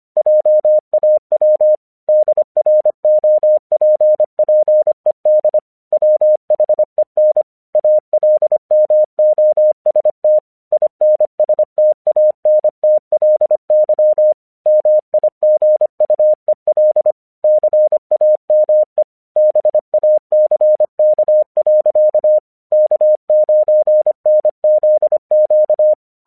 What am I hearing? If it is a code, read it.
Morse code, heard as a keyed tone, about 25 wpm, 610 Hz. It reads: JAW DROPPED WHEN ALMOST INSTANTLY MIGUEL CAME BACK. K9NZQ
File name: 1 English 25wpm / AQC31